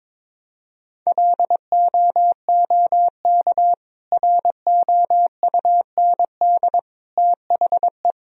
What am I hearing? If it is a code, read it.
Morse code, heard as a keyed tone, about 22 wpm, 700 Hz. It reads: LOOK ROUND THE
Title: Sentences from Top 1000 Words - Review Part 1 - 22wpm